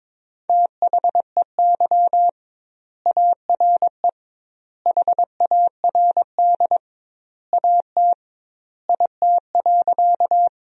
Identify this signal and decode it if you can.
Morse code, heard as a keyed tone, about 22 wpm, 700 Hz. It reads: THEY ARE HARD AT IT.
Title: 2x Word-Spacing - Sentences from Top 300 Words - 22wpm